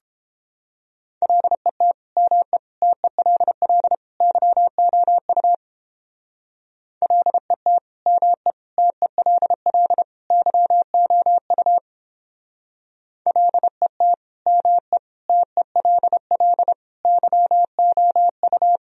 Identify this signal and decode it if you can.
Morse code, heard as a keyed tone, about 29 wpm, 700 Hz. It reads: LET ME TELL YOU LET ME TELL YOU LET ME TELL YOU